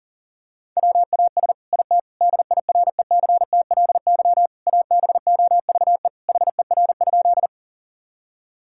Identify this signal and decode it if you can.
Morse code, heard as a keyed tone, about 40 wpm, 700 Hz. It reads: WAS IT DIRECTLY ABOVE HER?
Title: Sentences from Top 1000 Words - Review Part 5 - 40wpm